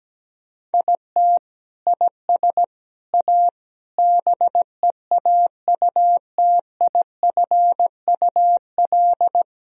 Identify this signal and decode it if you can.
Morse code, heard as a keyed tone, about 17 wpm, 700 Hz. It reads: IT IS A BEAUTIFUL
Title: Sentences from Top 1000 Words - Review Part 2 - 17wpm